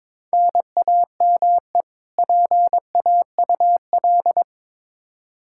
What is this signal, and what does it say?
Morse code, heard as a keyed tone, about 22 wpm, 700 Hz. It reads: NAME PAUL
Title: Practice QSO - Exchange 1 - Beginning - 22wpm